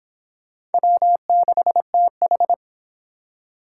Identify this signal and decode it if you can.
Morse code, heard as a keyed tone, about 25 wpm, 700 Hz. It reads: W6TH